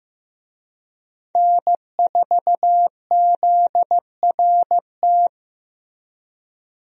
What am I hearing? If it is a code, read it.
Morse code, heard as a keyed tone, about 15 wpm, 700 Hz. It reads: N4ZRT
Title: USA Call Signs - 5 Characters - Repeated 3x - 15wpm